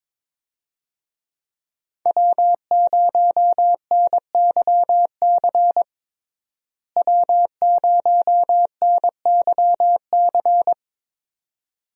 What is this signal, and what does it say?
Morse code, heard as a keyed tone, about 22 wpm, 700 Hz. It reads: W0NYC W0NYC
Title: USA Call Signs - 5 Characters - Repeated 2x - 22wpm